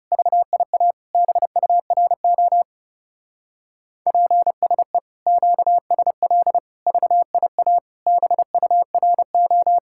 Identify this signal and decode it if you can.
Morse code, heard as a keyed tone, about 38 wpm, 700 Hz. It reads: VIA BURO PSE QSL VIA BURO